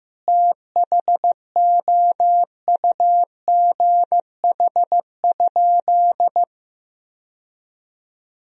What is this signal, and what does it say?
Morse code, heard as a keyed tone, about 15 wpm, 700 Hz. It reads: THOUGH?
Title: Sentences from Top 1000 Words - Review Part 2 - 15wpm